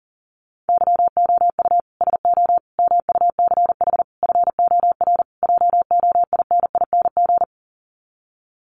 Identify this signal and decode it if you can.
Morse code, heard as a keyed tone, about 40 wpm, 700 Hz. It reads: YOU SO MUCH FOR JOINING
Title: Sets of 5 Words Encore2 40wpm